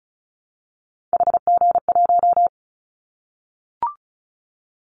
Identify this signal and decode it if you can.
Morse code, heard as a keyed tone, about 36 wpm, 700 Hz. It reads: HG1
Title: Callsign Prefixes for Top 25 DXCC - 35wpm